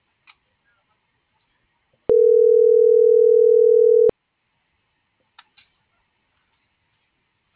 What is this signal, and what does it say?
Ringback tone (North America), 440 + 480 Hz.